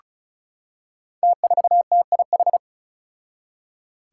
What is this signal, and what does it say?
Morse code, heard as a keyed tone, about 36 wpm, 700 Hz. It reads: T4TIH
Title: Five Challenging Letter-Number Combinations 35wpm